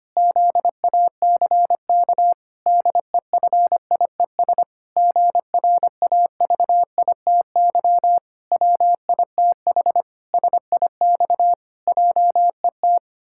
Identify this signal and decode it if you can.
Morse code, heard as a keyed tone, about 25 wpm, 700 Hz. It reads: ZACK DEFIES GRAVITY WITH SIX JET
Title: Pangram Challenge v2 - 25wpm